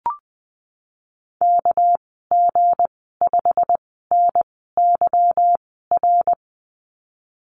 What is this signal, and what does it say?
Morse code, heard as a keyed tone, about 20 wpm, spaced as for 15 wpm, 700 Hz. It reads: KG5NYR
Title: 238 Lesson 38g - Letter q - Call Signs 20wpm